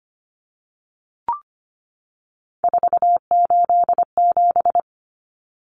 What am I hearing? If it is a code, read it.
Morse code, heard as a keyed tone, about 25 wpm, 690 Hz. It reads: 487
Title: Numbers - 3-Digit 25wpm